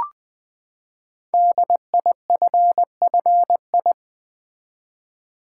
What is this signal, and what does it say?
Morse code, heard as a keyed tone, about 20 wpm, 700 Hz. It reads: DIFFI
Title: Build A Word - 5 Letters At A Time - 10-Letter Words - 20wpm